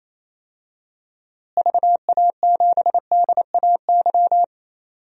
Morse code, heard as a keyed tone, about 28 wpm, 700 Hz. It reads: VA7DAY
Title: DXCC-100 Call Signs - 6 Characters - Repeated 3x - 28wpm